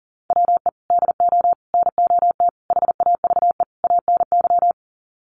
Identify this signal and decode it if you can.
Morse code, heard as a keyed tone, about 37 wpm, 700 Hz. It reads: WE DO NOT HAVE ANY
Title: Sets of 5 Words 40wpm